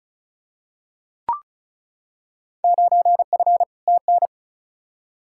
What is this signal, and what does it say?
Morse code, heard as a keyed tone, about 34 wpm, 700 Hz. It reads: 9F TN
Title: Field Day Exchanges Encore - 35wpm